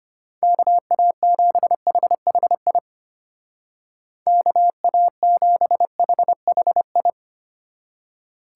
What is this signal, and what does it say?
Morse code, heard as a keyed tone, about 30 wpm, 700 Hz. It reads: KA7HHI KA7HHI
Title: Speed Racing - USA Call Signs 25wpm